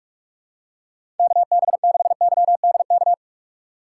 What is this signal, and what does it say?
Morse code, heard as a keyed tone, about 42 wpm, 700 Hz. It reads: KB6YDK
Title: USA Call Signs - 6 Characters - 3x Character Spacing - 45wpm